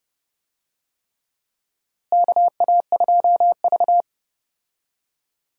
Morse code, heard as a keyed tone, about 30 wpm, 700 Hz. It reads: KA2V
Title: USA Call Signs - 4 Characters - Repeated 3x - 30wpm